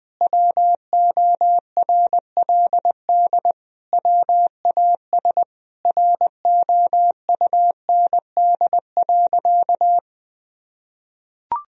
Morse code, heard as a keyed tone, about 20 wpm, 700 Hz. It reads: WORLD WAS ROUND.
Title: Sentences from Top 1000 Words - Review Part 1 - 20wpm